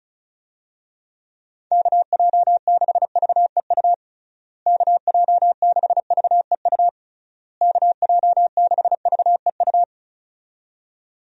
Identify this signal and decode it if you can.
Morse code, heard as a keyed tone, about 34 wpm, 700 Hz. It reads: KJ6VEU KJ6VEU KJ6VEU
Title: USA Call Signs - 6 Characters - Repeated 3x - 35wpm